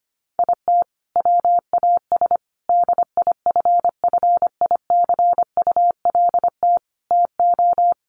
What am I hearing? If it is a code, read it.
Morse code, heard as a keyed tone, about 25 wpm, 700 Hz. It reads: IT WAS DIFFICULT TO
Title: Sentences from Top 1400 Words 25wpm